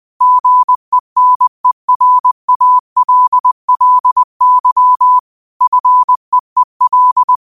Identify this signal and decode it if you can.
Morse code, heard as a keyed tone, about 20 wpm, 1000 Hz. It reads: GENERALLY FEEL
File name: 20wpm